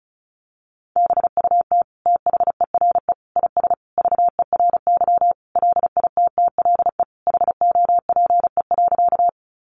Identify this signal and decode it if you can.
Morse code, heard as a keyed tone, about 34 wpm, 700 Hz. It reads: BUT THERE IS VERY LITTLE HOPE.
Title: Sentences from Top 1000 Words 35wpm